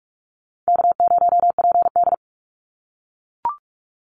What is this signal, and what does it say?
Morse code, heard as a keyed tone, about 44 wpm, 700 Hz. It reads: K0PD
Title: Call Signs - 45wpm